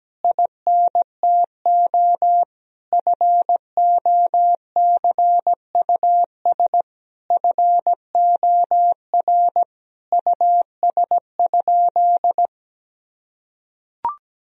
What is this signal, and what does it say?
Morse code, heard as a keyed tone, about 17 wpm, 700 Hz. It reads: INTO FOCUS FOR US?
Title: Sentences from Top 1000 Words - Review Part 4 - 17wpm